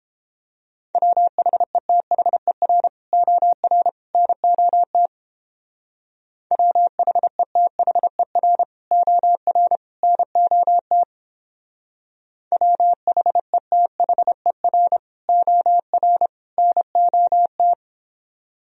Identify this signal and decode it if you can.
Morse code, heard as a keyed tone, about 31 wpm, 700 Hz. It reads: WHETHER OR NOT WHETHER OR NOT WHETHER OR NOT